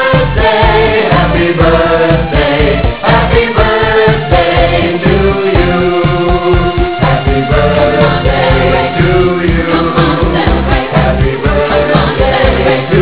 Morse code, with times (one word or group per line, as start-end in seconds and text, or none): none